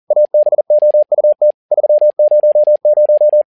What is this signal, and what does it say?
Morse code, heard as a keyed tone, about 40 wpm, 600 Hz. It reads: ABOUT 300
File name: learning center / tower trouble / tower touble 40wpm